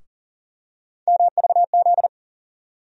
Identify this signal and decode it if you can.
Morse code, heard as a keyed tone, about 40 wpm, 700 Hz. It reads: MVZ
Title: USA Call Sign Postfixes - 3 Letters 40wpm